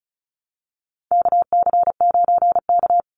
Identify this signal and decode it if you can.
Morse code, heard as a keyed tone, about 34 wpm, 700 Hz. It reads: KC9K
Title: Call Signs - 35wpm